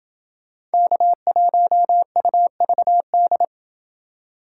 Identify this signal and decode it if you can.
Morse code, heard as a keyed tone, about 29 wpm, 700 Hz. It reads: K1UVD